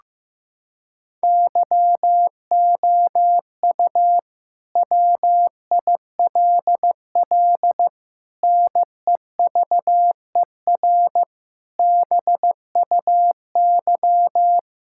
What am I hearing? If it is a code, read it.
Morse code, heard as a keyed tone, about 15 wpm, 700 Hz. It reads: YOU WILL NEVER BUY
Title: Sentences from Top 1000 Words - Review Part 5 - 15wpm